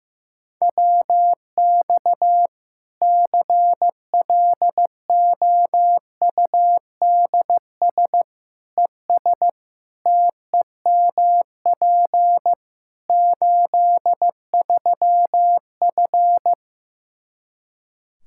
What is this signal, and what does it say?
Morse code, heard as a keyed tone, about 15 wpm, 700 Hz. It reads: WX CLOUDS ES TEMP 83F
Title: Practice QSO - Exchange 2 - Info - 15wpm